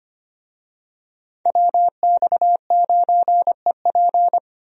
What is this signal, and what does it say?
Morse code, heard as a keyed tone, about 25 wpm, 690 Hz. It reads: WX9EP
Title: USA Call Signs - 5 Characters - 25wpm